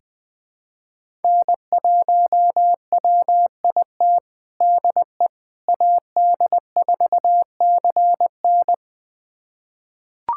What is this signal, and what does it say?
Morse code, heard as a keyed tone, about 20 wpm, 700 Hz. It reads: N1WIT DE AD4CN
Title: USA Callsign DE Callsign - 5 Characters - 20wpm